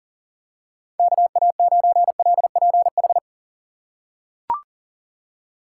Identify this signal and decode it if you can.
Morse code, heard as a keyed tone, about 40 wpm, 700 Hz. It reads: KA9LPH
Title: USA Call Signs - 6 Characters - 2x Character Spacing - 40wpm